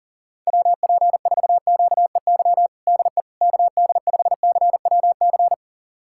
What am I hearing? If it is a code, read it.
Morse code, heard as a keyed tone, about 40 wpm, 700 Hz. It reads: WP4QEY DE KD5CWC
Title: USA Callsign DE Callsign 40wpm